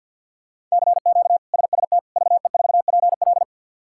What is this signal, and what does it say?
Morse code, heard as a keyed tone, about 49 wpm, 700 Hz. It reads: CQ SST VE4PL
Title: K1USN SST QSO Practice - 50wpm